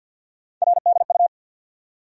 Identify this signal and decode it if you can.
Morse code, heard as a keyed tone, about 53 wpm, 700 Hz. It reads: RDU
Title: USA Call Sign Postfixes - 3 Letters 50wpm